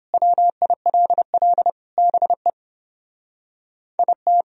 Morse code, heard as a keyed tone, about 30 wpm, 700 Hz. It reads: WILL BE IT